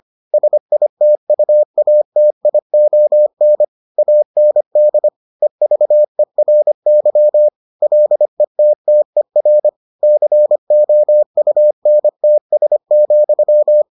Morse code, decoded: SITUATION AND EVERY LETTER COUNTS,

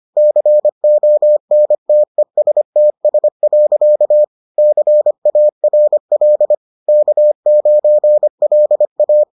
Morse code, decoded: CONTESTS. CARL K9LA

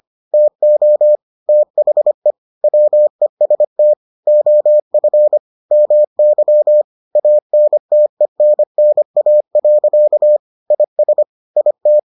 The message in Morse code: TO THE WEST OF MY ANTENNA. IS IT

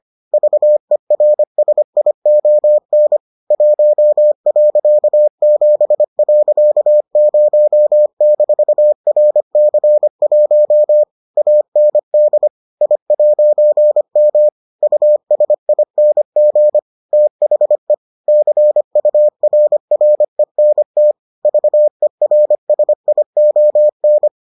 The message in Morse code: VERSION 1.7.0-RC1 AND I'M USING THE CURRENT VERSION

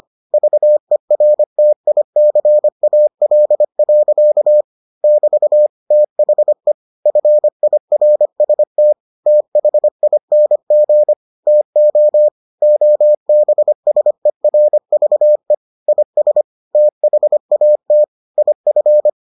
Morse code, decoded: VERTICAL. <BT> THE FIRST THING TO OBSERVE IS THAT IF

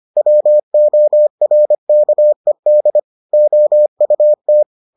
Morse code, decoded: WORKED OUT

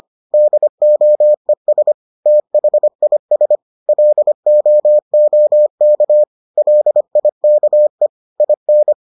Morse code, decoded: DOES THIS LOOK LIKE IN

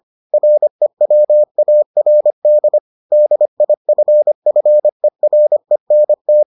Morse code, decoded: REWARD DIFFERENT

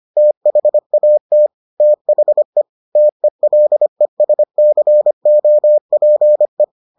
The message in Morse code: THAT THE TELESCOPE

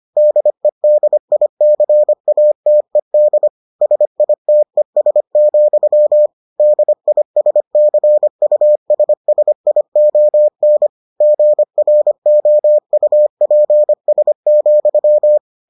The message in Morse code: DEDICATED SITES, DISCUSSION GROUPS,

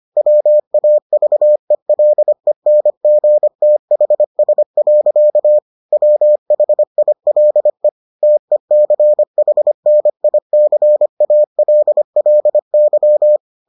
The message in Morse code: WAVELENGTHS. WHILE TECHNICALLY

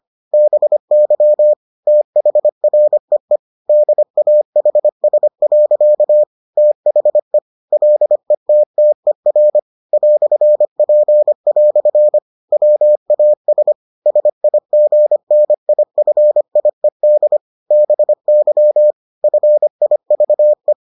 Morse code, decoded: BY THREE DAHS. THE LETTER "P" WAS SIGNIFIED BY FIVE

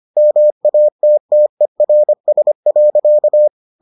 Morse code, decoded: MATTERS.

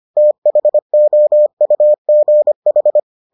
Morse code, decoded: THOUGH

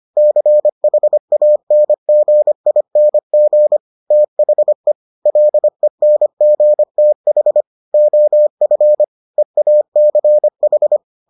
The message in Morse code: CHANGING THE LENGTH OF EACH

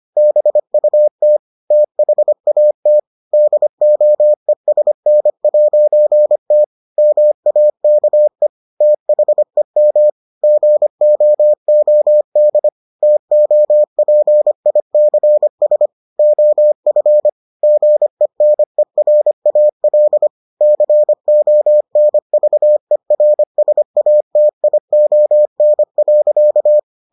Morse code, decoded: BUT THAT DOESN'T MAKE THEM GOOD TOPICS OF GENERAL CONVERSATION.

